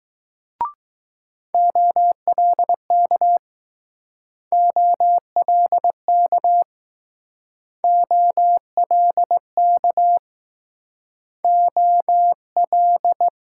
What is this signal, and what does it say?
Morse code, heard as a keyed tone, about 20 wpm, 700 Hz. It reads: OLK OLK OLK O L